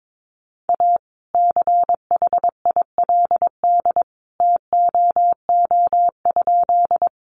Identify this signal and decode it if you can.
Morse code, heard as a keyed tone, about 22 wpm, 700 Hz. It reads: A CHILD TOO?
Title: Sentences from Top 800 Words 22wpm